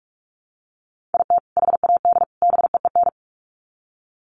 Morse code, heard as a keyed tone, about 44 wpm, 700 Hz. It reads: IT HAD BEEN
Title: Sets of 3 Words 45wpm